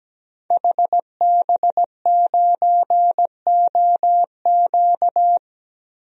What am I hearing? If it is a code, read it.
Morse code, heard as a keyed tone, about 17 wpm, 700 Hz. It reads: HB9OQ
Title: DXCC-100 Call Signs - 5 Characters - Repeated 3x - 17wpm